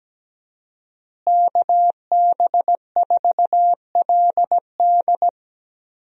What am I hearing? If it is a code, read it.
Morse code, heard as a keyed tone, about 17 wpm, 700 Hz. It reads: KB4LD
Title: USA Call Signs - 5 Characters - Repeated 2x - 17wpm